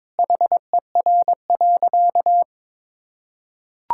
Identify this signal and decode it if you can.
Morse code, heard as a keyed tone, about 22 wpm, 700 Hz. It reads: HER.
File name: practice / 22wpm / Sentences with 4-Word Phrases - 22wpm